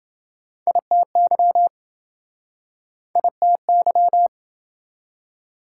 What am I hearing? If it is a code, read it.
Morse code, heard as a keyed tone, about 26 wpm, 700 Hz. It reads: ITY ITY